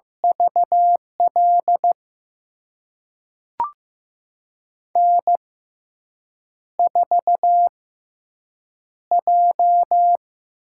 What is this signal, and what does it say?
Morse code, heard as a keyed tone, about 15 wpm, spaced as for 7 wpm, 700 Hz. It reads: VL N4J
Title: USA Call Signs - 5 Characters - 6x Character Spacing - 15wpm